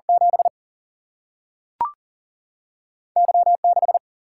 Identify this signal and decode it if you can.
Morse code, heard as a keyed tone, about 39 wpm, 700 Hz. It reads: 7 Y 6
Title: Single-Letter Single-Number 40wpm